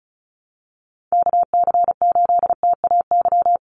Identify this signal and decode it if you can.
Morse code, heard as a keyed tone, about 34 wpm, 700 Hz. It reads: KC8TAY
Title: USA Call Signs Repeated 2x 35wpm